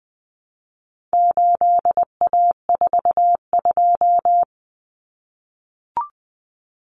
Morse code, heard as a keyed tone, about 20 wpm, 700 Hz. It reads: 8A42
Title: Common Cut Numbers - 4-Digit 20wpm